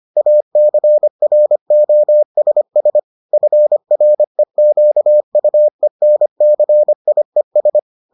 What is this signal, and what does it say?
Morse code, heard as a keyed tone, about 25 wpm, 600 Hz. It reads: ACROSS FREQUENCIES